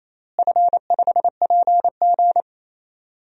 Morse code, heard as a keyed tone, about 28 wpm, 700 Hz. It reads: F5PG
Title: DXCC-100 Call Signs - 4 Characters - Repeated 2x - 28wpm